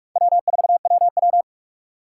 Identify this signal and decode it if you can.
Morse code, heard as a keyed tone, about 46 wpm, 700 Hz. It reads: W4WW
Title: USA Call Signs - 4 Characters - 4x Character Spacing - 45wpm